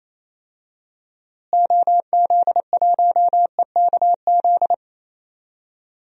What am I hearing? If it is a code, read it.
Morse code, heard as a keyed tone, about 28 wpm, 700 Hz. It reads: OZ1EKZ
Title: DXCC-100 Call Signs - 6 Characters - Repeated 3x - 28wpm